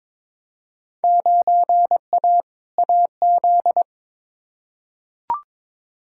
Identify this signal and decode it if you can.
Morse code, heard as a keyed tone, about 22 wpm, 700 Hz. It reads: 9A AZ
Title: Field Day Exchanges - 22wpm